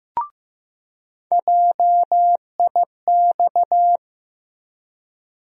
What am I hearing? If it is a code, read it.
Morse code, heard as a keyed tone, about 15 wpm, 700 Hz. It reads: JIX
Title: USA Call Sign Postfixes - 3 Letters 15wpm